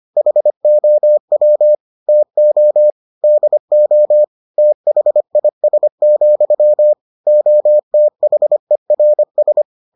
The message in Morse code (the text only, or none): HOW TO DO THIS, OTHERS